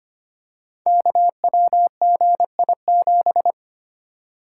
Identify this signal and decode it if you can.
Morse code, heard as a keyed tone, about 25 wpm, 700 Hz. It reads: KWGI7
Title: Code Groups - 5 Letter:Numbers - 25wpm